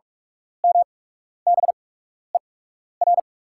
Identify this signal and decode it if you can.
Morse code, heard as a keyed tone, about 44 wpm, spaced as for 12 wpm, 690 Hz. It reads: MBER
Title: 8x Character Spacing - Top 100 Words - 45wpm